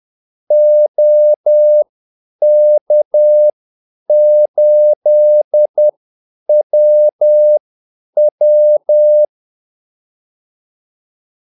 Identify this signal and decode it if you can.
Morse code, heard as a keyed tone, about 10 wpm, spaced as for 8 wpm, 600 Hz. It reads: OK8WW
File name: Callsigns-10-8wpm